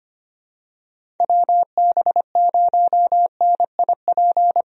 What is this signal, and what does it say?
Morse code, heard as a keyed tone, about 25 wpm, 700 Hz. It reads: WB0NIP
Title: USA Call Signs - 6 Characters - Repeated 2x - 25wpm